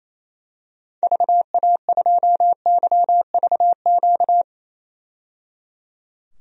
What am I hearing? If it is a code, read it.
Morse code, heard as a keyed tone, about 28 wpm, 700 Hz. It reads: VA2YVQ
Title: DXCC-100 Call Signs - 6 Characters - 28wpm